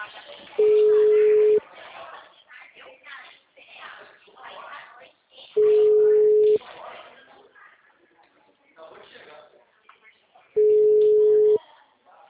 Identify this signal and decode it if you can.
Ringback tone (Europe, 425 Hz), with speech.